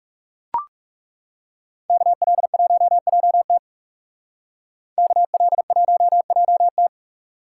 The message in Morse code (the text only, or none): KL1JT KL1JT